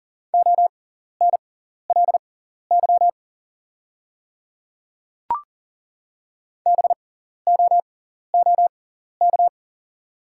Morse code, heard as a keyed tone, about 40 wpm, spaced as for 14 wpm, 690 Hz. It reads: ONLY BOOK
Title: 6x Character Spacing - Top 300 Words - 40wpm